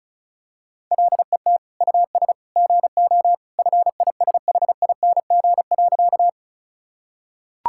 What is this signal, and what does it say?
Morse code, heard as a keyed tone, about 34 wpm, 700 Hz. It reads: LET US GO FISHING.
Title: Sentences from Top 1000 Words - Review Part 5 - 35wpm